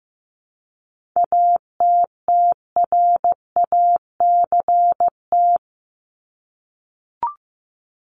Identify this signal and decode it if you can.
Morse code, heard as a keyed tone, about 15 wpm, 700 Hz. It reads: ATTRACT
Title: Top 1501-1750 Words 15wpm